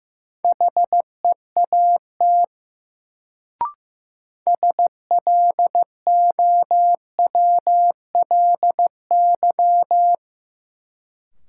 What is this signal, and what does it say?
Morse code, heard as a keyed tone, about 15 wpm, 700 Hz. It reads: HEAT SLOWLY
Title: Sentences from Top 1100 Words - New Word Drill Alt - 15wpm